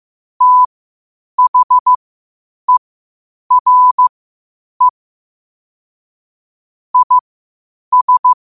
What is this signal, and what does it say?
Morse code, heard as a keyed tone, about 15 wpm, spaced as for 8 wpm, 1000 Hz. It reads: THERE IS
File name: qotdfiles / 8wpm